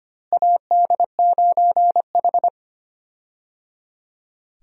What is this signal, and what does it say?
Morse code, heard as a keyed tone, about 25 wpm, 700 Hz. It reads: AD9H
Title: USA Call Signs - 4 Characters - Repeated 3x - 25wpm